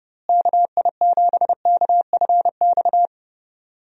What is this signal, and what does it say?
Morse code, heard as a keyed tone, about 30 wpm, 700 Hz. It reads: KI7KFX
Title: USA Call Signs - 6 Characters - Repeated 3x - 30wpm